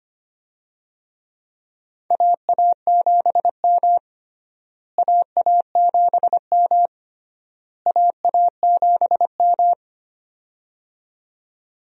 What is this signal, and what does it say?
Morse code, heard as a keyed tone, about 25 wpm, 700 Hz. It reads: AA7M AA7M AA7M
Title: USA Call Signs - 4 Characters - Repeated 3x - 25wpm